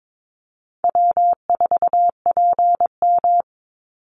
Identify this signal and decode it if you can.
Morse code, heard as a keyed tone, about 22 wpm, 700 Hz. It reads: W4PM
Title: Easy Call Signs Repeated 3x - 22wpm